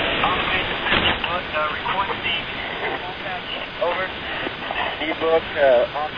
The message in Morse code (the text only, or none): none